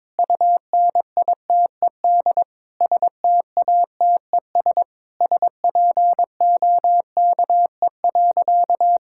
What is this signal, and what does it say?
Morse code, heard as a keyed tone, about 22 wpm, 700 Hz. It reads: UNITED STATES SPOKE.
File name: practice / 22wpm / Sentences with 4-Word Phrases - 22wpm